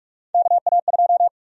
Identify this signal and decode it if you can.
Morse code, heard as a keyed tone, about 42 wpm, 700 Hz. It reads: KA2
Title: USA Call Sign Prefixes - 45wpm